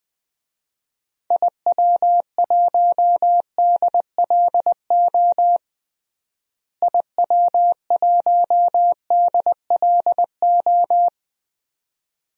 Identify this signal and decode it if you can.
Morse code, heard as a keyed tone, about 20 wpm, 700 Hz. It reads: IW1DLO IW1DLO
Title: DXCC-100 Call Signs - 6 Characters - Repeated 2x - 20wpm